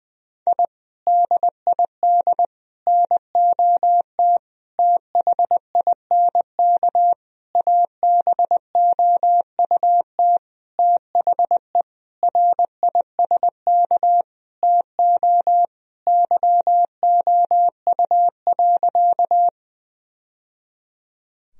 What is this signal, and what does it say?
Morse code, heard as a keyed tone, about 21 wpm, 700 Hz. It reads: I DID NOT THINK ABOUT THE RISK TO YOU.
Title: Sentences from Top 1000 Words - Review Part 4 - 20wpm